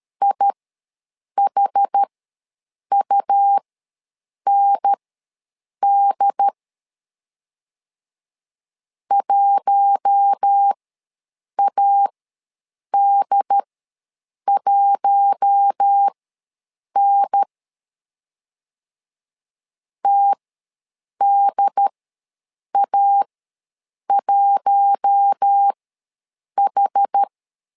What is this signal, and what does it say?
Morse code, heard as a keyed tone, about 13 wpm, spaced as for 7 wpm, 800 Hz. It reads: IHUND 1AD1N TDA1H
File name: LEZIONI / 4 Q1 ws 12wpm 2129